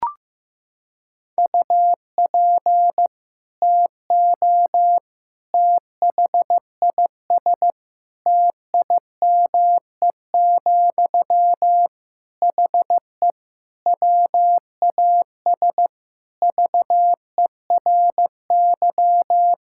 Morse code, heard as a keyed tone, about 15 wpm, 700 Hz. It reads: UP TO THIS TIME, HE WAS VERY
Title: Sentences from Top 1000 Words - Review Part 6 - 15wpm